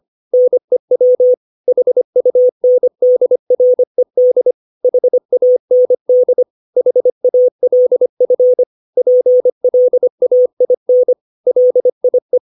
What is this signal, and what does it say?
Morse code, heard as a keyed tone, about 25 wpm, 500 Hz. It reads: NEW HUNDRED HAND HALF PLAIN LIE
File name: 632 words / 632 Words 25WPM